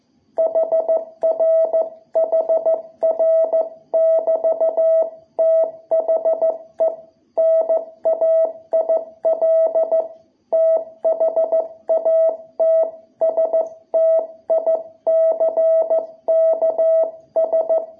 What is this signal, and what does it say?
Morse code, heard as a keyed tone, about 14 wpm, 660 Hz. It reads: HR HR <BT> THE NAIL THAT STICKS
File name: jh8cbh / 2021cw